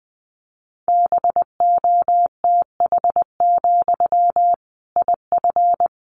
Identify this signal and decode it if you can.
Morse code, heard as a keyed tone, about 20 wpm, 700 Hz. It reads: BOTH, IF
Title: Sentences from Top 1500 Words 20wpm